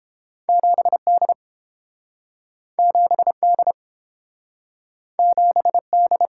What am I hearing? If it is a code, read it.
Morse code, heard as a keyed tone, about 32 wpm, 700 Hz. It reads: 7D 7D 7D